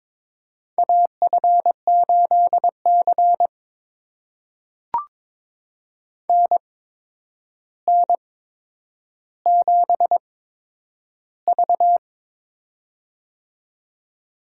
Morse code, heard as a keyed tone, about 23 wpm, spaced as for 10 wpm, 700 Hz. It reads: AF8C NN7V
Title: USA Call Signs - 4 Characters - 8x Character Spacing - 22wpm